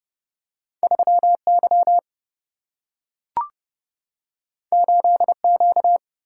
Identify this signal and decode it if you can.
Morse code, heard as a keyed tone, about 30 wpm, 700 Hz. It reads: 3 Y 8 Q
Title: Single-Letter Single-Number 30wpm